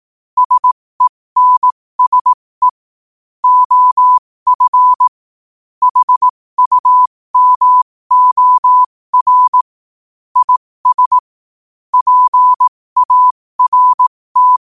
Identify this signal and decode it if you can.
Morse code, heard as a keyed tone, about 18 wpm, spaced as for 15 wpm, 1000 Hz. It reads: SENSE OF HUMOR IS PART
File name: qotdfiles / 15wpm